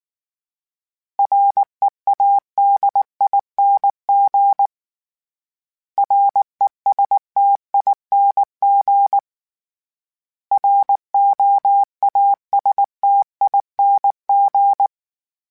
Morse code, heard as a keyed tone, about 19 wpm, 800 Hz. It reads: READING RESTING ROASTING
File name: Special INGs / Special INGs-mp3s / Sngl Ing @18